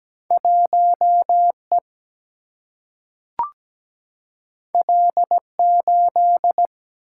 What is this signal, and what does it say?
Morse code, heard as a keyed tone, about 17 wpm, 700 Hz. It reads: 1E L8
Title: Single-Letter Single-Number 17wpm